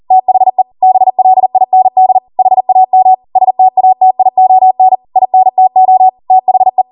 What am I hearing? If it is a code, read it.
Morse code, heard as a keyed tone, about 40 wpm, 750 Hz. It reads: THE BLIND HAM STATION INTO THE